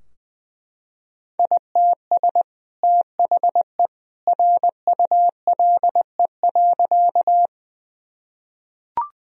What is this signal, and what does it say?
Morse code, heard as a keyed tone, about 19 wpm, 700 Hz. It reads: ITS THE RULE.
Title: Sentences from Top 1000 Words - Review Part 2 - 20wpm